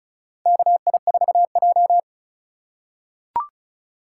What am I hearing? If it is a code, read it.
Morse code, heard as a keyed tone, about 34 wpm, 700 Hz. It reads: KI4J
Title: USA Call Signs - 4 Characters - 2x Character Spacing - 35wpm